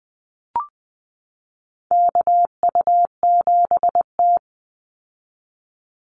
Morse code, heard as a keyed tone, about 20 wpm, 700 Hz. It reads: KU7T
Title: Call Signs Repeated 3x - 20wpm